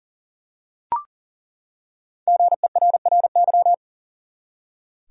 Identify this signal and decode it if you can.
Morse code, heard as a keyed tone, about 40 wpm, 700 Hz. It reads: GERRY
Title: USA Common Names - Max 5 Letters - 40wpm